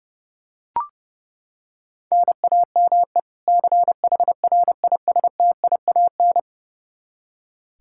Morse code, heard as a keyed tone, about 31 wpm, 700 Hz. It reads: NAME CHRISTIAN
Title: Practice QSO - Exchange 1 - Beginning - 30wpm